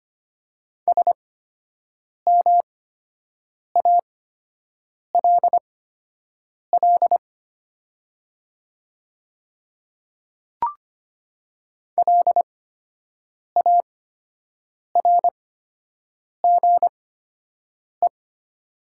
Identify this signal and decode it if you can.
Morse code, heard as a keyed tone, about 25 wpm, spaced as for 7 wpm, 700 Hz. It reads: SMALL LARGE
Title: 8x Character Spacing - Top 200 Words - 25wpm